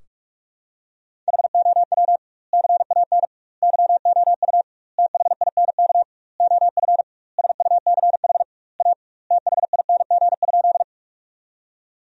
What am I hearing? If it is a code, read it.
Morse code, heard as a keyed tone, about 46 wpm, 700 Hz. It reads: HOW CAN YOU THINK OF SUCH A THING?
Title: 2x Word-Spacing - Sentences from Top 200 Words - 45wpm